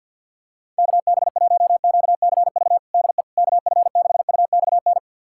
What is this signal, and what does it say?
Morse code, heard as a keyed tone, about 48 wpm, 700 Hz. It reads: KB1QCV DE KF6UXN
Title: USA Callsign DE Callsign - 6 Characters - 50wpm